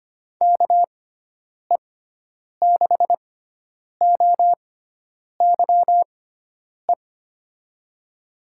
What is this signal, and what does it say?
Morse code, heard as a keyed tone, about 25 wpm, spaced as for 9 wpm, 700 Hz. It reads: KE6OYE
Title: USA Call Signs - 6 Characters - 6x Character Spacing - 25wpm